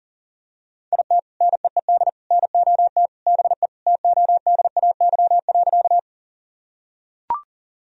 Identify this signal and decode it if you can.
Morse code, heard as a keyed tone, about 37 wpm, 700 Hz. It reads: IT NEED NOT BE TODAY.
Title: Sentences from Top 1000 Words - Review Part 4 - 40wpm